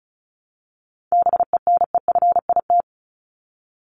Morse code, heard as a keyed tone, about 34 wpm, 700 Hz. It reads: BENEFIT
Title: Top 751-1000 Words 35wpm